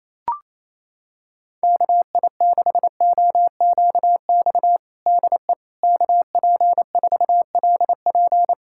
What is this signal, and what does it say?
Morse code, heard as a keyed tone, about 28 wpm, 700 Hz. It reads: KI6OQX DE KP4LP
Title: USA Callsign DE Callsign 28wpm